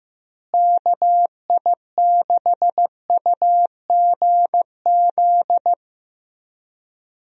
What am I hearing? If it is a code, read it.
Morse code, heard as a keyed tone, about 15 wpm, 700 Hz. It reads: KI6UGZ
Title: USA Call Signs - 6 Characters - Repeated 2x - 15wpm